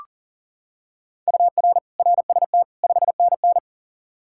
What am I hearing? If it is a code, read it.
Morse code, heard as a keyed tone, about 37 wpm, 700 Hz. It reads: UR RST 5NN
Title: K1USN SST QSO Practice 40wpm